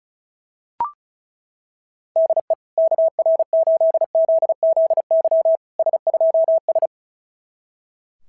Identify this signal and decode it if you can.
Morse code, heard as a keyed tone, about 33 wpm, 660 Hz. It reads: DE KR8ZZY S2S
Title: SOTA QSO Practice 35wpm